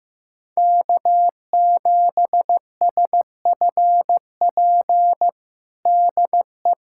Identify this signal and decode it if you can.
Morse code, heard as a keyed tone, about 15 wpm, 700 Hz. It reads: K7SFP DE
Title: USA Callsign DE Callsign 15wpm